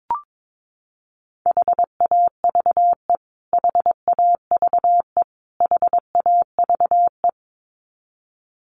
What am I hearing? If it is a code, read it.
Morse code, heard as a keyed tone, about 22 wpm, 700 Hz. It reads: HAVE HAVE HAVE
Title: Top 500 Words Repeated 22wpm